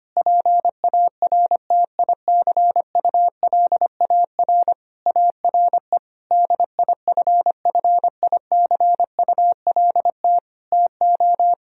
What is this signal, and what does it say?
Morse code, heard as a keyed tone, about 25 wpm, 700 Hz. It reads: PARTICULAR ARE DIFFICULT TO
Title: Sentences from Top 1000 Words - Review Part 4 - 25wpm